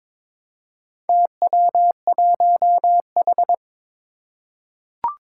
Morse code, heard as a keyed tone, about 22 wpm, 700 Hz. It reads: TW1H
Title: Four Challenging Letter-Number Combinations 22wpm